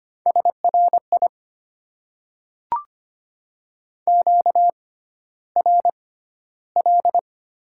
Morse code, heard as a keyed tone, about 25 wpm, spaced as for 13 wpm, 700 Hz. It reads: SRI QRL
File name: Top 90 QSO Elements - 6x Character Spacing - 25wpm